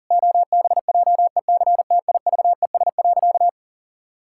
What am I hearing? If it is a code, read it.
Morse code, heard as a keyed tone, about 40 wpm, 700 Hz. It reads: OBJECTIVES.
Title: Pangram Challenge v2 - 40wpm